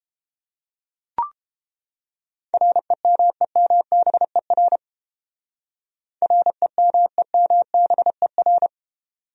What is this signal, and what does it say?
Morse code, heard as a keyed tone, about 30 wpm, 700 Hz. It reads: REMEMBER REMEMBER